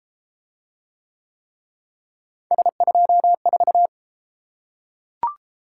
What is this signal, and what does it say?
Morse code, heard as a keyed tone, about 31 wpm, 700 Hz. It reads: S24